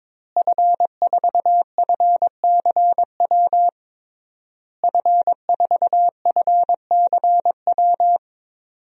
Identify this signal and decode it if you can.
Morse code, heard as a keyed tone, about 22 wpm, 700 Hz. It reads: F4FCW F4FCW
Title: DXCC-100 Call Signs - 5 Characters - Repeated 3x - 22wpm